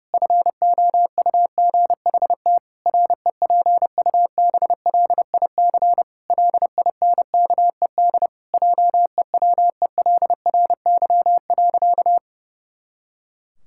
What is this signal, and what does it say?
Morse code, heard as a keyed tone, about 32 wpm, 700 Hz. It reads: FOUGHT REPUBLIC LINKED JEWELRY.
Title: Pangram Challenge v2 - 30wpm